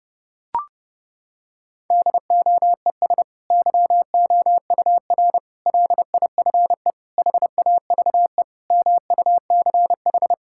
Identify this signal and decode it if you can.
Morse code, heard as a keyed tone, about 32 wpm, 700 Hz. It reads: DOES YOUR LIFE HAVE MUCH
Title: Sentences from Top 1000 Words - Review Part 6 - 30wpm